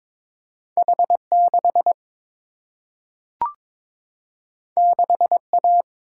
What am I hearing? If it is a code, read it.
Morse code, heard as a keyed tone, about 22 wpm, 700 Hz. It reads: H6 6A
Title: Single-Letter Single-Number 22wpm